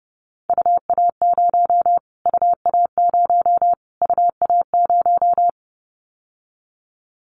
Morse code, heard as a keyed tone, about 30 wpm, 700 Hz. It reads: UA0 UA0 UA0
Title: Callsign Prefixes for Top 25 DXCC - Repeated 3x - 30wpm